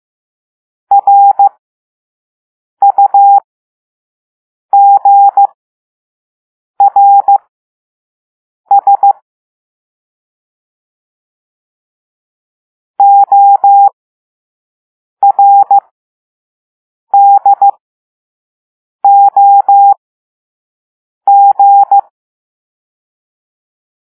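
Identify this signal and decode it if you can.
Morse code, heard as a keyed tone, about 15 wpm, spaced as for 5 wpm, 800 Hz. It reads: RUGRS ORDOG
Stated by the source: Morse code lesson 5 drill of random five-letter groups using all letters learned so far.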